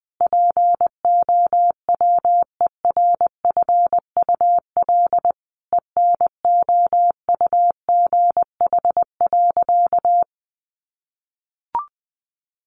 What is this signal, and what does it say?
Morse code, heard as a keyed tone, about 20 wpm, 700 Hz. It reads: POWERFUL ENOUGH.
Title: Sentences from Top 1300 Words 20wpm